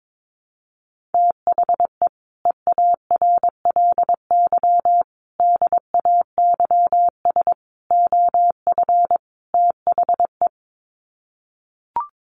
Morse code, heard as a keyed tone, about 22 wpm, 700 Hz. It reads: THE EARLY DAYS OF THE
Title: Sets of 5 Words Encore 22wpm